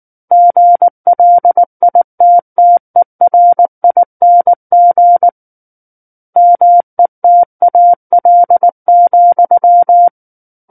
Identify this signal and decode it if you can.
Morse code, heard as a keyed tone, about 19 wpm, 700 Hz. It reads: GLITTERING METAL,
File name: Book One / 10-Chapter 10 - 19 WPM